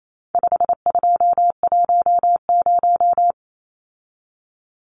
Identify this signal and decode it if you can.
Morse code, heard as a keyed tone, about 28 wpm, 700 Hz. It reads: 5210
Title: Numbers - 4-Digit 28wpm